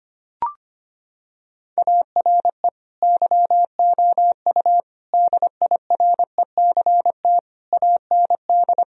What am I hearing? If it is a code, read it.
Morse code, heard as a keyed tone, about 25 wpm, 700 Hz. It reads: ARE YOU DIRECT AND
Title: Sentences from Top 1000 Words - Review Part 4 - 25wpm